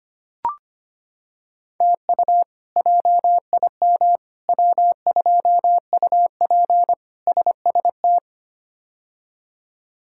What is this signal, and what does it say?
Morse code, heard as a keyed tone, about 25 wpm, 700 Hz. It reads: TU JIM W2UP SST
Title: K1USN SST QSO Practice - 25wpm